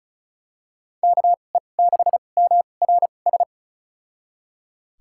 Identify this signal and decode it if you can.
Morse code, heard as a keyed tone, about 34 wpm, spaced as for 25 wpm, 700 Hz. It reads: KE6MRS
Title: USA Call Signs - 6 Characters - 2x Character Spacing - 35wpm